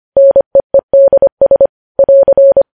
Morse code, heard as a keyed tone, about 25 wpm, 550 Hz. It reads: NEEDS <AR>
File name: podcast / News headlines-1x-25x25